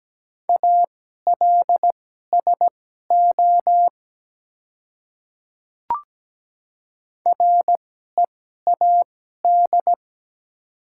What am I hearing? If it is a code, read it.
Morse code, heard as a keyed tone, about 17 wpm, spaced as for 12 wpm, 700 Hz. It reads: ALSO READ
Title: 2x Character Spacing - Top 200 Words - 17wpm